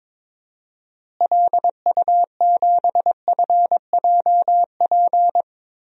Morse code, heard as a keyed tone, about 22 wpm, 700 Hz. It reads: LU7FJP
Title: DXCC-100 Call Signs - 6 Characters - Repeated 3x - 22wpm